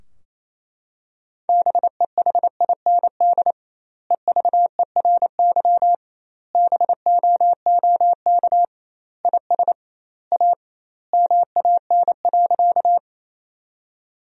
Morse code, heard as a keyed tone, about 27 wpm, 700 Hz. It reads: BEHIND EVERY BOOK IS A MAN.
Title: 2x Word-Spacing - Sentences from Top 500 Words - 28wpm